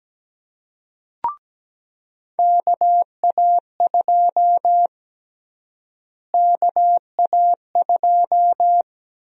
Morse code, heard as a keyed tone, about 17 wpm, 700 Hz. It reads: KA2 KA2
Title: USA Call Sign Prefixes - Repeated 2x - 17wpm